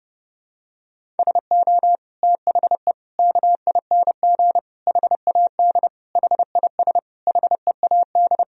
Morse code, heard as a keyed tone, about 30 wpm, 700 Hz. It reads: SO THE KING HAD HIS HEAD
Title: Sentences from Top 1000 Words - Review Part 1 - 30wpm